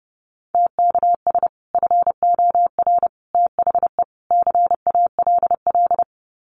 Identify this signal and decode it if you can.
Morse code, heard as a keyed tone, about 30 wpm, 700 Hz. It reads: TKS FOR THE CALL
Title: Common QSO Phrases 30wpm